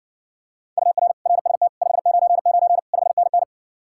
Morse code, heard as a keyed tone, about 60 wpm, 700 Hz. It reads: UR RST 599 5NN